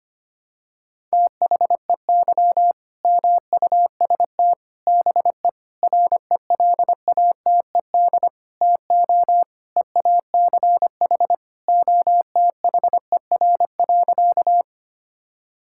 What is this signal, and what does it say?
Morse code, heard as a keyed tone, about 25 wpm, 700 Hz. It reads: THEY MUST BE RELATED TO EACH OTHER.
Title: Sentences from Top 1000 Words - Review Part 5 - 25wpm